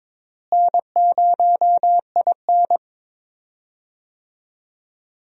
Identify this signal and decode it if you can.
Morse code, heard as a keyed tone, about 22 wpm, 700 Hz. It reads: N0IN
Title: USA Call Signs - 4 Characters - Repeated 2x - 22wpm